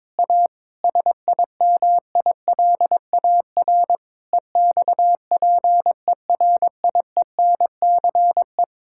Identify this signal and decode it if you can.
Morse code, heard as a keyed tone, about 23 wpm, 700 Hz. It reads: A SIMILAR EXPERIENCE
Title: Sentences from Top 1000 Words - Review Part 5 - 22wpm